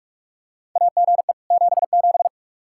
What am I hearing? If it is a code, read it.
Morse code, heard as a keyed tone, about 46 wpm, 700 Hz. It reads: AGE 77
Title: Practice QSO - Exchange 3 - More Info - 45wpm